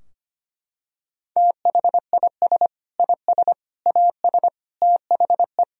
Morse code, heard as a keyed tone, about 25 wpm, 700 Hz. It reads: THIS IS AS THE
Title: Sentences from Top 1000 Words - Review Part 3 - 25wpm